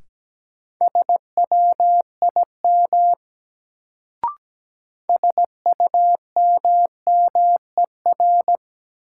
Morse code, heard as a keyed tone, about 17 wpm, 700 Hz. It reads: SWIM SUMMER
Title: Sentences from Top 1000 Words - Review Part 6 - New Word Drill - 17wpm